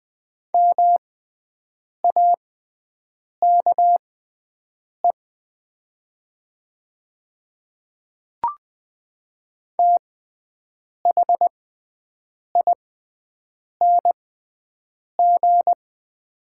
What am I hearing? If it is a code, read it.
Morse code, heard as a keyed tone, about 20 wpm, spaced as for 7 wpm, 700 Hz. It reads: MAKE THING
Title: 6x Character Spacing - Top 300 Words - 20wpm